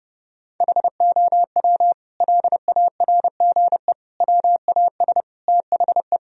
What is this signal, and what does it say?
Morse code, heard as a keyed tone, about 30 wpm, 700 Hz. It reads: HOW LARGE WAS THE
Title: Sentences from Top 1000 Words - Review Part 4 - 30wpm